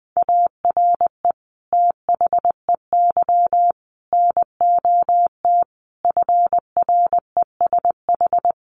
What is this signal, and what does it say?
Morse code, heard as a keyed tone, about 20 wpm, 700 Hz. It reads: ARE THEY NOT FRESH
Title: Sentences from Top 1200 Words 20wpm